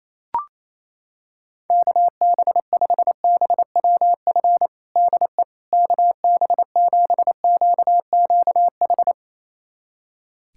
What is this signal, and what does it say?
Morse code, heard as a keyed tone, about 28 wpm, 700 Hz. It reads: KB5BWF DE KB7QQH
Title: USA Callsign DE Callsign 28wpm